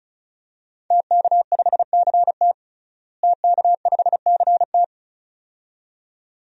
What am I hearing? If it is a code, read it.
Morse code, heard as a keyed tone, about 34 wpm, 700 Hz. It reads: TK5CT TK5CT
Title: DXCC-100 Call Signs - 5 Characters - Repeated 2x - 35wpm